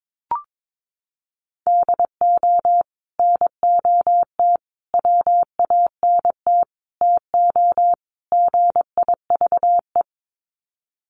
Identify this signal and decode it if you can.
Morse code, heard as a keyed tone, about 22 wpm, 700 Hz. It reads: DO NOT WANT TO GIVE
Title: Sets of 5 Words Encore 22wpm